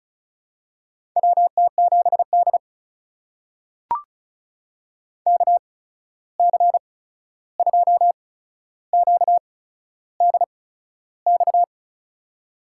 Morse code, heard as a keyed tone, about 36 wpm, spaced as for 13 wpm, 700 Hz. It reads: WT7D KC2QDX
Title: USA Call Signs - 8x Character Spacing - 35wpm